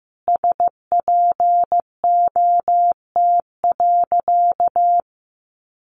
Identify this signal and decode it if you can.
Morse code, heard as a keyed tone, about 15 wpm, 700 Hz. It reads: SPOT.
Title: Sentences from Top 1100 Words 15wpm